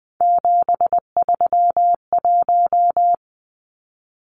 Morse code, heard as a keyed tone, about 20 wpm, 700 Hz. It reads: 731
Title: Numbers - 3-Digit 20wpm